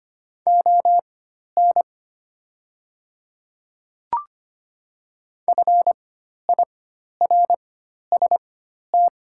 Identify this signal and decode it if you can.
Morse code, heard as a keyed tone, about 25 wpm, spaced as for 12 wpm, 700 Hz. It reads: ON FIRST
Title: 4x Character Spacing - Top 100 Words - 25wpm